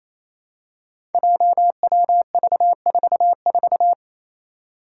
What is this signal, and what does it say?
Morse code, heard as a keyed tone, about 28 wpm, 700 Hz. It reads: JWV44
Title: Five Challenging Letter-Number Combinations 28wpm